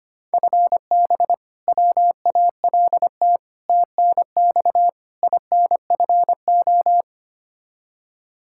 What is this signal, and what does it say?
Morse code, heard as a keyed tone, about 25 wpm, 700 Hz. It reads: FB WALT TNX INFO
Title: Practice QSO - Exchange 3 - More Info - 25wpm